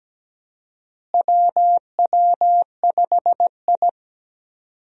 Morse code, heard as a keyed tone, about 17 wpm, 700 Hz. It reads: WW5I
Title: USA Call Signs - 4 Characters - Repeated 3x - 17wpm